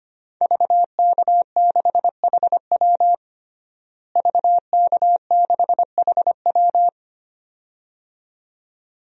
Morse code, heard as a keyed tone, about 25 wpm, 700 Hz. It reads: VK6HW VK6HW
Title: DXCC-100 Call Signs - 5 Characters - Repeated 2x - 25wpm